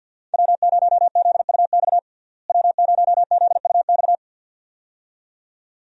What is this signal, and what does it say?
Morse code, heard as a keyed tone, about 50 wpm, 700 Hz. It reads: W0ZUX W0ZUX
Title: USA Call Signs - 5 Characters - Repeated 2x - 50wpm